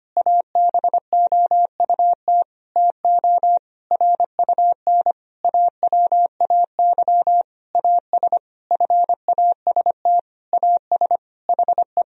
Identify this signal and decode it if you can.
Morse code, heard as a keyed tone, about 25 wpm, 700 Hz. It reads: ABOUT TO RUN AWAY AS FAST AS HE
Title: Sentences from Top 1000 Words - Review Part 2 - 25wpm